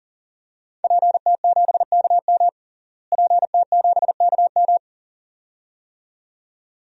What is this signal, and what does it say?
Morse code, heard as a keyed tone, about 40 wpm, 700 Hz. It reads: PT7KM PT7KM
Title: DXCC-100 Call Signs - 5 Characters - Repeated 2x - 40wpm